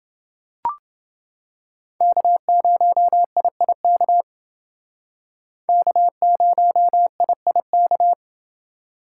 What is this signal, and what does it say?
Morse code, heard as a keyed tone, about 31 wpm, 700 Hz. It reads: K0IIK K0IIK